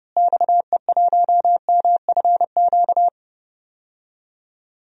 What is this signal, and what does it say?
Morse code, heard as a keyed tone, about 30 wpm, 700 Hz. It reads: XE1MFQ
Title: DXCC-100 Call Signs - 6 Characters - Repeated 2x - 30wpm